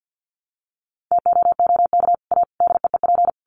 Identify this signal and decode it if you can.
Morse code, heard as a keyed tone, about 50 wpm, 700 Hz. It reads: TOOK A DEEP
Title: Sets of 5 Words Encore 50wpm